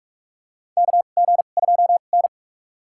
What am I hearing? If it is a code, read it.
Morse code, heard as a keyed tone, about 47 wpm, spaced as for 33 wpm, 700 Hz. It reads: KG2N
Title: USA Call Signs - 4 Characters - 2x Character Spacing - 45wpm